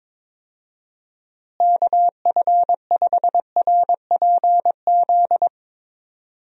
Morse code, heard as a keyed tone, about 22 wpm, 700 Hz. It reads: KF5RPZ
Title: USA Call Signs - 6 Characters - Repeated 2x - 22wpm